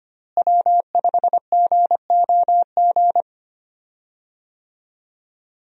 Morse code, heard as a keyed tone, about 25 wpm, 700 Hz. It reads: W5GOG
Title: USA Call Signs - 5 Characters - Repeated 2x - 25wpm